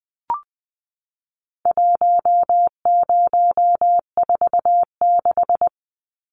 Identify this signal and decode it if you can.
Morse code, heard as a keyed tone, about 20 wpm, 700 Hz. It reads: 1046
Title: Numbers - 4-Digit 20wpm